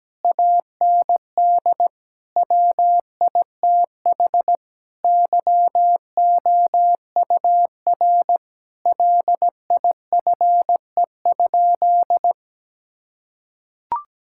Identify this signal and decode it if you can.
Morse code, heard as a keyed tone, about 17 wpm, 700 Hz. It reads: AND WITH YOUR LIFE?
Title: Sentences from Top 1000 Words - Review Part 6 - 17wpm